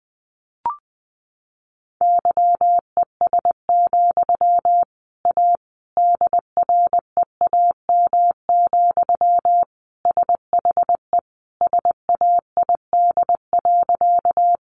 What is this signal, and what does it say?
Morse code, heard as a keyed tone, about 20 wpm, 700 Hz. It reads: YES, A DREAM, SHE SAID.
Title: Sentences from Top 1500 Words 20wpm